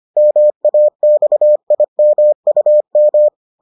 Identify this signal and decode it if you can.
Morse code, heard as a keyed tone, about 25 wpm, 600 Hz. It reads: MAXIMUM